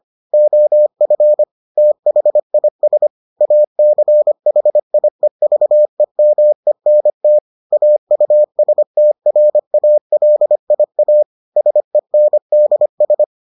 Morse code, heard as a keyed tone, about 25 wpm, 600 Hz. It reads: OF THIS ACHIEVEMENT AUSTRALIA SENDS